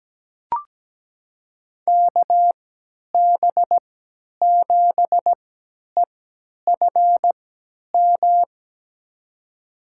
Morse code, heard as a keyed tone, about 17 wpm, spaced as for 10 wpm, 700 Hz. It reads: KB7EFM
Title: USA Call Signs - 3x Character Spacing - 17wpm